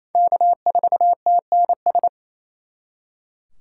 Morse code, heard as a keyed tone, about 36 wpm, 700 Hz. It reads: K4TNS